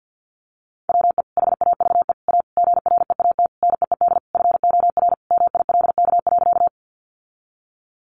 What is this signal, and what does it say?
Morse code, heard as a keyed tone, about 50 wpm, 700 Hz. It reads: WE HAVE A GREAT NEED FOR MILK.
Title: Sentences from Top 1500 Words 50wpm